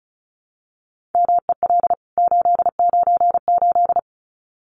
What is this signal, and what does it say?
Morse code, heard as a keyed tone, about 34 wpm, 700 Hz. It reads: MEL 898
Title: CWT Exchange 35wpm